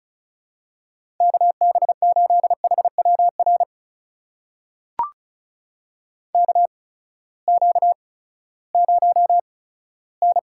Morse code, heard as a keyed tone, about 36 wpm, spaced as for 18 wpm, 700 Hz. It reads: KB8HWR KQ0N
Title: USA Call Signs - 8x Character Spacing - 35wpm